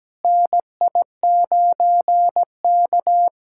Morse code, heard as a keyed tone, about 17 wpm, 700 Hz. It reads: NI9K
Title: USA Callsign DE Callsign - 4 Characters - 17wpm